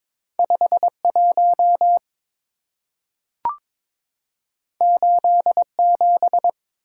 Morse code, heard as a keyed tone, about 22 wpm, 700 Hz. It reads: 51 87
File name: practice / 22wpm / Two Letter-Number Combinations - 22wpm